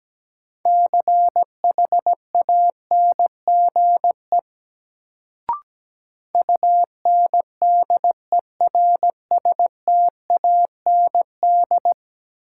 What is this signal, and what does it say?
Morse code, heard as a keyed tone, about 17 wpm, 700 Hz. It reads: CHANGE UNDERSTAND
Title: Sentences from Top 600 Words - New Word Drill - 17wpm